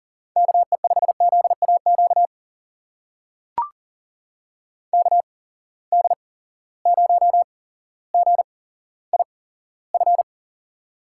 Morse code, heard as a keyed tone, about 40 wpm, spaced as for 17 wpm, 700 Hz. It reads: KE5ZAQ KD0GIF
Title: USA Call Signs - 6 Characters - 8x Character Spacing - 40wpm